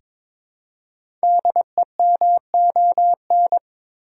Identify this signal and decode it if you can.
Morse code, heard as a keyed tone, about 22 wpm, 690 Hz. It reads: DEMON
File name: 5 Letter Words - First 3 Letters Given - 22wpm